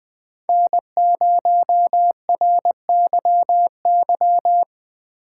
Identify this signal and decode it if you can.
Morse code, heard as a keyed tone, about 20 wpm, 700 Hz. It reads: N0RYY
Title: USA Call Signs - 4x Character Spacing - 20wpm